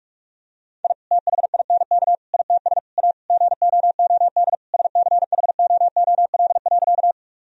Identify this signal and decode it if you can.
Morse code, heard as a keyed tone, about 46 wpm, 700 Hz. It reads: I THINK ITS A GOOD SCHOOL.
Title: Sentences from Top 1000 Words - Review Part 1 - 45wpm